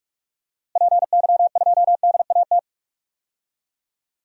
Morse code, heard as a keyed tone, about 41 wpm, 700 Hz. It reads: PY2DAT
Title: DXCC-100 Call Signs - 6 Characters - Repeated 3x - 45wpm